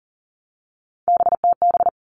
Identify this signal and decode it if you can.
Morse code, heard as a keyed tone, about 40 wpm, 700 Hz. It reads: BTB
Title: All Cut Numbers - 3-Digit 40wpm